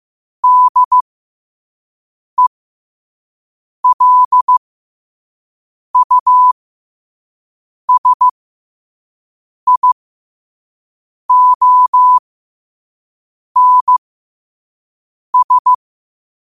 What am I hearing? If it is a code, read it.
Morse code, heard as a keyed tone, about 15 wpm, spaced as for 5 wpm, 1000 Hz. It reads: DELUSIONS
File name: qotdfiles / 5wpm